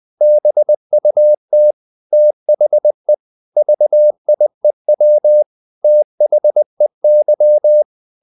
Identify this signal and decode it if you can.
Morse code, heard as a keyed tone, about 20 wpm, 600 Hz. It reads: BUT THE VIEW THEY
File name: CW / Practice / EnglishQuotations0320